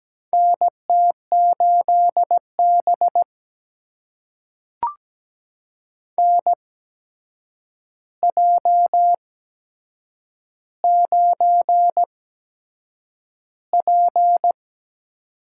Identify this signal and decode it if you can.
Morse code, heard as a keyed tone, about 17 wpm, spaced as for 7 wpm, 700 Hz. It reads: NT8B NJ9P
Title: USA Call Signs - 4 Characters - 8x Character Spacing - 17wpm